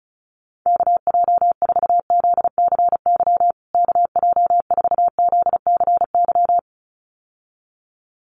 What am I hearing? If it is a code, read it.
Morse code, heard as a keyed tone, about 34 wpm, 700 Hz. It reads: KJ4ZCY KJ4ZCY
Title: USA Call Signs Repeated 2x 35wpm